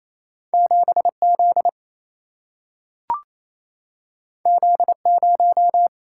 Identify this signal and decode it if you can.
Morse code, heard as a keyed tone, about 28 wpm, 700 Hz. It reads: 7 Z Z 0
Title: Single-Letter Single-Number 28wpm